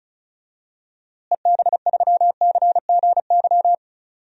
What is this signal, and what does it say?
Morse code, heard as a keyed tone, about 34 wpm, 700 Hz. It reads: EB3CGY
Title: DXCC-100 Call Signs - 6 Characters - Repeated 3x - 35wpm